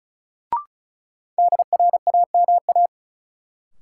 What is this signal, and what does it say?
Morse code, heard as a keyed tone, about 36 wpm, 700 Hz. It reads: DRAMA
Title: Sentences from Top 1500 Words - New Word Drill - 35wpm